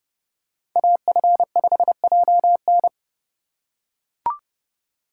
Morse code, heard as a keyed tone, about 31 wpm, 700 Hz. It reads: AF5JN
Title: USA Call Signs - 4x Character Spacing - 30wpm